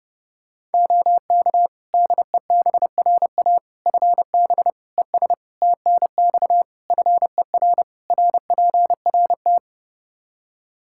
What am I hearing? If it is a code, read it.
Morse code, heard as a keyed tone, about 30 wpm, 700 Hz. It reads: OK DEBRA FB ES TNX FER RPRT
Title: Practice QSO - Exchange 2 - Info - 30wpm